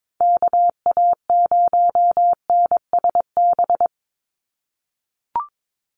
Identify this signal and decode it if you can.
Morse code, heard as a keyed tone, about 22 wpm, 700 Hz. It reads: KA0NSB
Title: USA Call Signs 22wpm